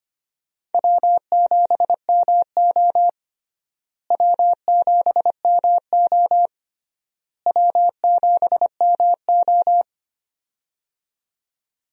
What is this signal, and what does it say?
Morse code, heard as a keyed tone, about 25 wpm, 700 Hz. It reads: W7MO W7MO W7MO
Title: USA Call Signs - 4 Characters - Repeated 3x - 25wpm